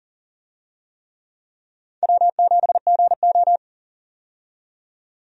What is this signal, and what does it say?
Morse code, heard as a keyed tone, about 40 wpm, 700 Hz. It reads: W7GO
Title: USA Call Signs - 4 Characters - 40wpm